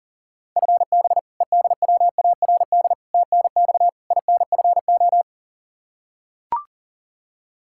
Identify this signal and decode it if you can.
Morse code, heard as a keyed tone, about 37 wpm, 700 Hz. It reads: FB EDWARD TNX INFO
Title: Practice QSO - Exchange 3 - More Info - 40wpm